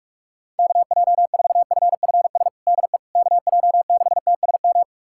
Morse code, heard as a keyed tone, about 46 wpm, 700 Hz. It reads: KJ4FFS DE KJ6TSM
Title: USA Callsign DE Callsign - 6 Characters - 45wpm